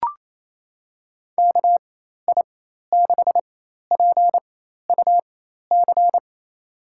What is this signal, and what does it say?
Morse code, heard as a keyed tone, about 28 wpm, spaced as for 13 wpm, 700 Hz. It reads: KI6PUC
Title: USA Call Signs - 6 Characters - 4x Character Spacing - 28wpm